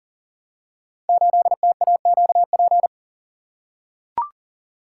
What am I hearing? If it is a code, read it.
Morse code, heard as a keyed tone, about 39 wpm, 700 Hz. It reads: 8TAQP
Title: Code Groups - 5 Letter:Numbers - 40wpm